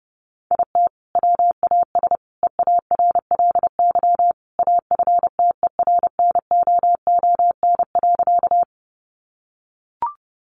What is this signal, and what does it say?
Morse code, heard as a keyed tone, about 30 wpm, 700 Hz. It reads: IT WAS EARLY AFTERNOON.
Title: Sentences from Top 1500 Words 30wpm